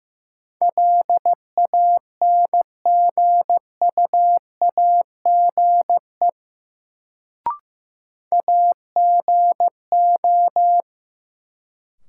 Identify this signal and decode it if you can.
Morse code, heard as a keyed tone, about 16 wpm, 700 Hz. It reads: LANGUAGE AGO
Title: Sentences from Top 500 Words - New Word Drill - 15wpm